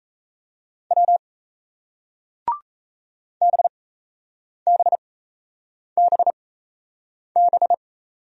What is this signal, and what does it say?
Morse code, heard as a keyed tone, about 38 wpm, spaced as for 8 wpm, 700 Hz. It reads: W BBBB